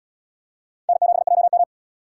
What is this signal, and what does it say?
Morse code, heard as a keyed tone, about 69 wpm, 700 Hz. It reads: N7JM